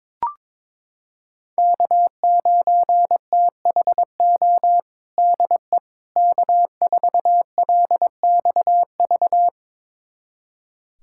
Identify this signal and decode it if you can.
Morse code, heard as a keyed tone, about 22 wpm, 700 Hz. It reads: K9THO DE K4LXV
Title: USA Callsign DE Callsign 22wpm